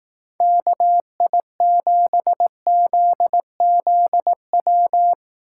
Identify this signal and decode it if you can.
Morse code, heard as a keyed tone, about 18 wpm, 700 Hz. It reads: KI7ZZW
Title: Speed Racing - USA Call Signs 15wpm